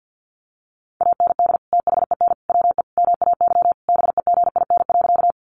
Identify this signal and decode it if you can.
Morse code, heard as a keyed tone, about 50 wpm, 700 Hz. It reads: AND THEN WE MAY BEGIN.
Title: Sentences from Top 300 Words 50wpm